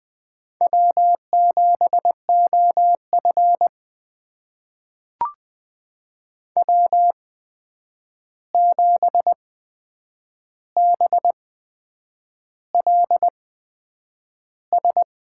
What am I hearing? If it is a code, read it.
Morse code, heard as a keyed tone, about 20 wpm, spaced as for 8 wpm, 700 Hz. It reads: W7OF W7BLS
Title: USA Call Signs - 8x Character Spacing - 20wpm